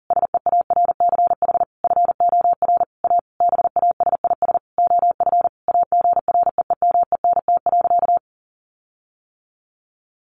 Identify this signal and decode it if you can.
Morse code, heard as a keyed tone, about 40 wpm, 700 Hz. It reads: SEARCH FOR A BASIS OF AGREEMENT.
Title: Sentences from Top 1500 Words 40wpm